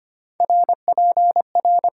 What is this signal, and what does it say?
Morse code, heard as a keyed tone, about 25 wpm, 690 Hz. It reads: RPR
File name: Top 90 QSO Elements - 8x Character Spacing - 25wpm